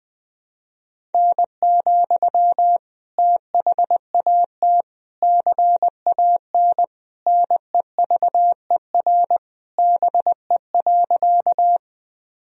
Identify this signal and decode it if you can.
Morse code, heard as a keyed tone, about 20 wpm, 700 Hz. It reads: N, THAT CAN NEVER BE.
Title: Sentences from Top 300 Words and Abbreviations 20wpm